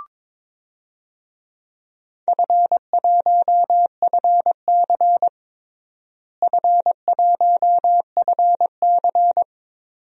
Morse code, heard as a keyed tone, about 22 wpm, 700 Hz. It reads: F1FC F1FC
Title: DXCC-100 Call Signs - 4 Characters - Repeated 2x - 22wpm